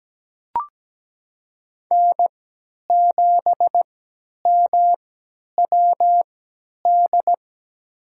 Morse code, heard as a keyed tone, about 17 wpm, spaced as for 10 wpm, 700 Hz. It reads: N7MWD
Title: USA Call Signs - 5 Characters - 3x Character Spacing - 17wpm